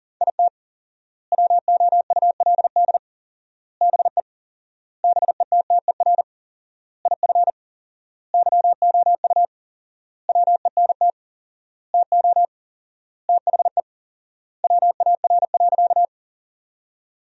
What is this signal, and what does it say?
Morse code, heard as a keyed tone, about 40 wpm, 700 Hz. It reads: IT WOULD BE BETTER IF YOU WENT TO THE WAR.
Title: 4x Word-Spacing - Sentences from Top 400 Words - 40wpm